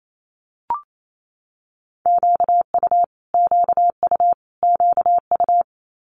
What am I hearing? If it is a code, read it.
Morse code, heard as a keyed tone, about 28 wpm, 700 Hz. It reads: QU QU QU
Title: Multi-letter Phonemes Repeated 3x 28wpm